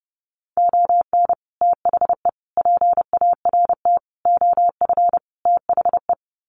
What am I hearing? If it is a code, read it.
Morse code, heard as a keyed tone, about 28 wpm, 700 Hz. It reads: ON THE PART OF THE
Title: Sets of 5 Words 30wpm